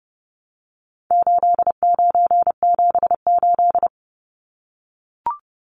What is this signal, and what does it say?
Morse code, heard as a keyed tone, about 30 wpm, 700 Hz. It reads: 8978
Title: Numbers - 4-Digit 30wpm